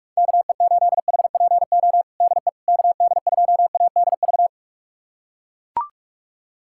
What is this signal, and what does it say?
Morse code, heard as a keyed tone, about 43 wpm, 700 Hz. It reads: KE8HPO DE KD2ADV
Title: USA Callsign DE Callsign - 6 Characters - 45wpm